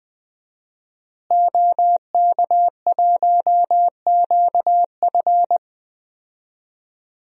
Morse code, heard as a keyed tone, about 21 wpm, 700 Hz. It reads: OK1QF
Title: DXCC-100 Call Signs - 5 Characters - Repeated 2x - 20wpm